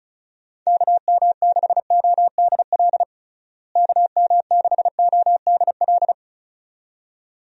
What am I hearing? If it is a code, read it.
Morse code, heard as a keyed tone, about 34 wpm, 700 Hz. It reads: KM6ODL KM6ODL
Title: USA Call Signs - 6 Characters - Repeated 2x - 35wpm